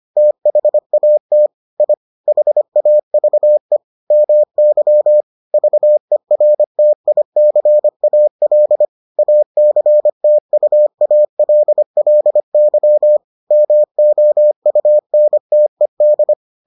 Morse code, heard as a keyed tone, about 25 wpm, 600 Hz. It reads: THAT I HAVE MY VERTICAL ACTUALLY MOUNTED